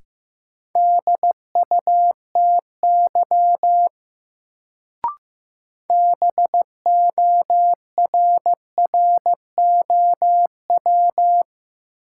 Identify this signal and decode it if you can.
Morse code, heard as a keyed tone, about 15 wpm, 700 Hz. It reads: DUTY BORROW
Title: Sentences from Top 1500 Words - New Word Drill Alt - 15wpm